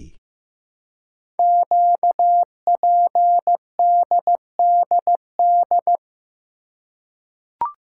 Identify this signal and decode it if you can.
Morse code, heard as a keyed tone, about 15 wpm, 700 Hz. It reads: QPDDD
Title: Code Groups - 5 Letters - 15wpm